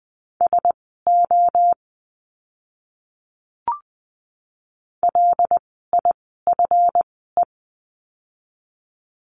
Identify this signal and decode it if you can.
Morse code, heard as a keyed tone, about 20 wpm, spaced as for 14 wpm, 700 Hz. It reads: SO LIFE
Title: 206 Lesson 33f - Letter j - Words 20wpm